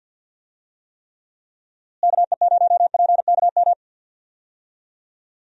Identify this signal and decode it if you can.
Morse code, heard as a keyed tone, about 52 wpm, 700 Hz. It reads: KE0PKM
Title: USA Call Signs - 6 Characters - Repeated 3x - 50wpm